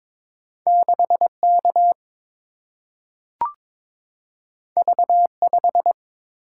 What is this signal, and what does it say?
Morse code, heard as a keyed tone, about 22 wpm, 700 Hz. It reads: 6K V5
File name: practice / 22wpm / Two Letter-Number Combinations - 22wpm